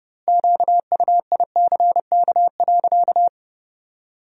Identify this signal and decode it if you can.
Morse code, heard as a keyed tone, about 30 wpm, 700 Hz. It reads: QUICK.
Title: 2x Word-Spacing - Sentences from Top 500 Words - 30wpm